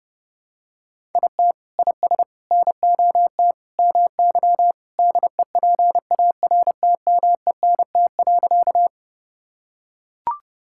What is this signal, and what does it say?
Morse code, heard as a keyed tone, about 30 wpm, 700 Hz. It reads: IT IS NOT MY DEPARTMENT.
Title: Sentences from Top 1000 Words - Review Part 5 - 30wpm